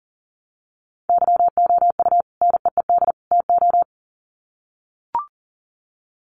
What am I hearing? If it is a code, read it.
Morse code, heard as a keyed tone, about 40 wpm, 700 Hz. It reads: YOU NEED TO
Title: Sets of 3 Words 40wpm